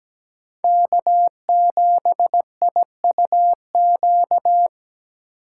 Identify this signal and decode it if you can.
Morse code, heard as a keyed tone, about 22 wpm, 700 Hz. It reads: K7IUQ